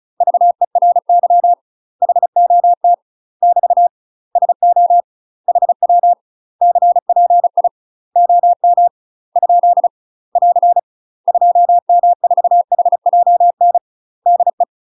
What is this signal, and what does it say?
Morse code, heard as a keyed tone, about 36 wpm, 700 Hz. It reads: VERY HOT <BT> SO HW CPI OM ? <AR> 2M4HJN DE
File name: UK MASTER QSOs TRAINING / 35WPM / UK MASTER QSOs 0011